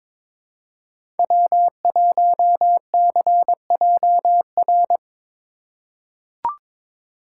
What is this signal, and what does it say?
Morse code, heard as a keyed tone, about 22 wpm, 700 Hz. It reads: W1CJR
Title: USA Call Signs - 5 Characters - 3x Character Spacing - 22wpm